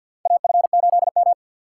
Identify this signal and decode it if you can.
Morse code, heard as a keyed tone, about 52 wpm, 700 Hz. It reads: A F 8 M
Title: USA Call Signs - 6x Character Spacing - 50wpm